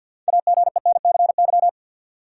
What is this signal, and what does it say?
Morse code, heard as a keyed tone, about 52 wpm, 700 Hz. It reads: AGENCY
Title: Six Letter Words - 50wpm